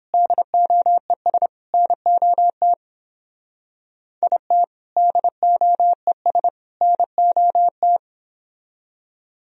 Morse code, heard as a keyed tone, about 29 wpm, 700 Hz. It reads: DOES NOT IT DOES NOT